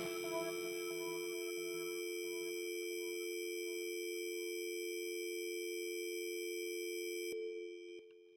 Dial tone (North America), 350 + 440 Hz.